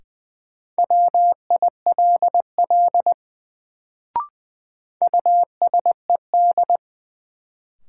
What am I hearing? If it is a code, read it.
Morse code, heard as a keyed tone, about 20 wpm, 700 Hz. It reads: WILL USED
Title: Sets of 2 Words - New Word Drill - 20wpm